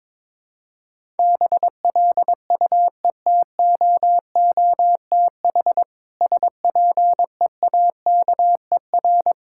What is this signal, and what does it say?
Morse code, heard as a keyed tone, about 22 wpm, 700 Hz. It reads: BLUETOOTH SPEAKER
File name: Difficult Words with Hints - 22wpm